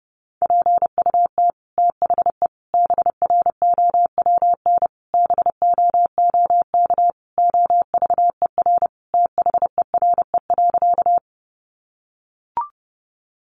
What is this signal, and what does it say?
Morse code, heard as a keyed tone, about 30 wpm, 700 Hz. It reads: PUT THE BROWN BOOK OVER THERE.
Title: Sentences from Top 1200 Words 30wpm